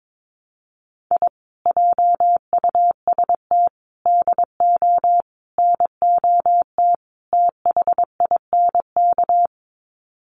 Morse code, heard as a keyed tone, about 22 wpm, 700 Hz. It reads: I JUST DO NOT THINK
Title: Sets of 5 Words Encore 22wpm